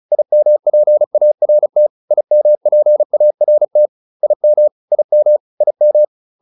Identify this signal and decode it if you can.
Morse code, heard as a keyed tone, about 36 wpm, 600 Hz. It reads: IMPART IMPART IM IM IM